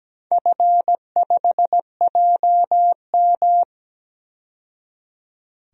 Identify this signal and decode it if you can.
Morse code, heard as a keyed tone, about 17 wpm, 700 Hz. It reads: F5JM
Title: DXCC-100 Call Signs - 4 Characters - 17wpm